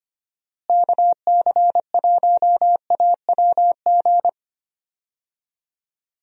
Morse code, heard as a keyed tone, about 25 wpm, 700 Hz. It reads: KC1AWG
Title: USA Call Signs - 6 Characters - Repeated 3x - 25wpm